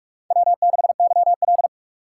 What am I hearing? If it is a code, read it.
Morse code, heard as a keyed tone, about 42 wpm, 700 Hz. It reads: W6YL
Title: USA Call Signs - 4x Character Spacing - 45wpm